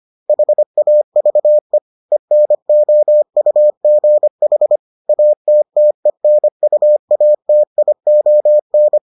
Morse code, heard as a keyed tone, about 25 wpm, 600 Hz. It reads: HAVE ENOUGH ATTENUATION